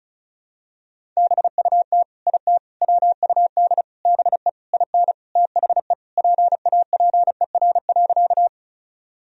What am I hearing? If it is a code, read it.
Morse code, heard as a keyed tone, about 34 wpm, 700 Hz. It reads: BUT IT WUD BE IN THE PAPER.
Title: Sentences from Top 300 Words and Abbreviations 35wpm